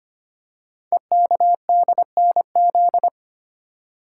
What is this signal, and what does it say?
Morse code, heard as a keyed tone, about 25 wpm, 700 Hz. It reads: EKDNZ
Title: Code Groups - 5 Letters - 25wpm